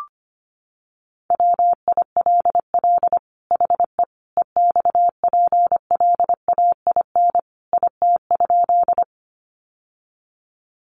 Morse code, heard as a keyed tone, about 25 wpm, 700 Hz. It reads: WILL HE EXPLAIN IT?
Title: Sentences from Top 1100 Words 25wpm